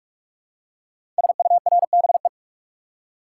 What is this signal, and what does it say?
Morse code, heard as a keyed tone, about 44 wpm, 700 Hz. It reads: SURBE
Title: Code Groups - 5 Letters - 45wpm